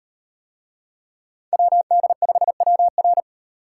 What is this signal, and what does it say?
Morse code, heard as a keyed tone, about 38 wpm, 700 Hz. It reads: WD5WR